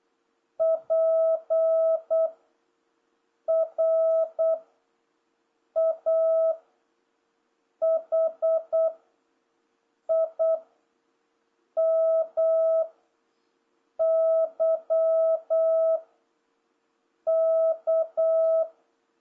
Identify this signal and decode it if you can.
Morse code, heard as a keyed tone, about 8 wpm, 650 Hz. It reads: P R A H I M Y K